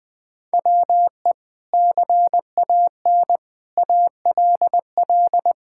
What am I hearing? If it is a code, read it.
Morse code, heard as a keyed tone, about 20 wpm, 700 Hz. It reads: WE CAN ALL
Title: Sentences with 6-Word Phrases - 20wpm